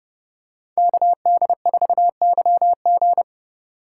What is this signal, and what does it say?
Morse code, heard as a keyed tone, about 27 wpm, 700 Hz. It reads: KD4YG